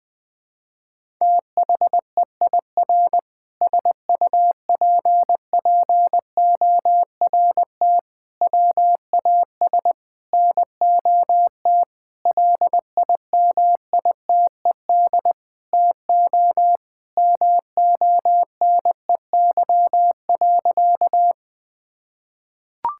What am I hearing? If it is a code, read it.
Morse code, heard as a keyed tone, about 20 wpm, 700 Hz. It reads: THEIR SUPPORT WAS NOT LIMITED TO MONEY.
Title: Sentences from Top 1000 Words - Review Part 6 - 20wpm